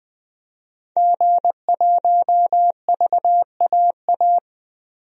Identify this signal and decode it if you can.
Morse code, heard as a keyed tone, about 20 wpm, 700 Hz. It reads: G1VAA
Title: DXCC-100 Call Signs - 5 Characters - 20wpm